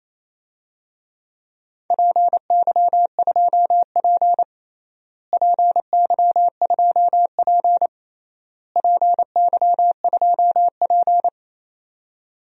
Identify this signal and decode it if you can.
Morse code, heard as a keyed tone, about 28 wpm, 700 Hz. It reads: PY2P PY2P PY2P
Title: DXCC-100 Call Signs - 4 Characters - Repeated 3x - 28wpm